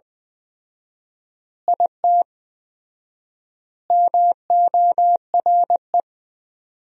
Morse code, heard as a keyed tone, about 20 wpm, 700 Hz. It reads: IT MORE
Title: 4x Word-Spacing - Sentences from Top 400 Words - 20wpm